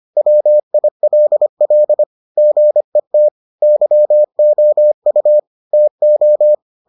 Morse code, heard as a keyed tone, about 25 wpm, 600 Hz. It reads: WILL GET YOU TO